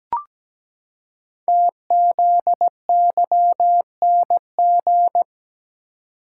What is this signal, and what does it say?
Morse code, heard as a keyed tone, about 17 wpm, 700 Hz. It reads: TZYNG
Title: Code Groups - 5 Letter:Numbers - 17wpm